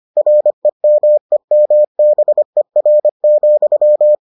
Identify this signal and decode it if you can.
Morse code, heard as a keyed tone, about 25 wpm, 600 Hz. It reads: REMEMBER,